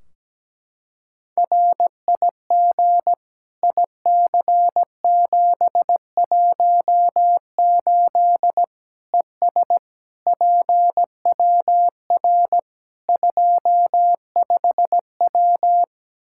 Morse code, heard as a keyed tone, about 17 wpm, 700 Hz. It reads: RIG IC718 ES PWR 25W
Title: Practice QSO - Exchange 2 - Info - 17wpm